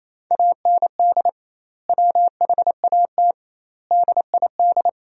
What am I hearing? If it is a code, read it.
Morse code, heard as a keyed tone, about 28 wpm, 700 Hz. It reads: AND WHAT DID
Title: 2x Word-Spacing - Sentences from Top 300 Words - 28wpm